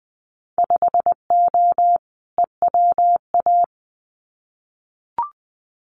Morse code, heard as a keyed tone, about 20 wpm, 700 Hz. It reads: 5O EWA
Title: Winter Field Day Exchanges Encore 20wpm